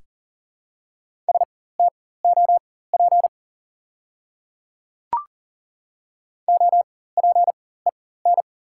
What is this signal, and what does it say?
Morse code, heard as a keyed tone, about 40 wpm, spaced as for 19 wpm, 700 Hz. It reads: STOP OPEN
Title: 4x Character Spacing - Top 300 Words - 40wpm